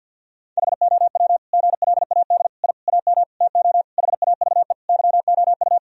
Morse code, heard as a keyed tone, about 50 wpm, 700 Hz. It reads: HOW GLAD I AM TO HAVE YOU
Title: Sentences from Top 1000 Words - Review Part 2 - 50wpm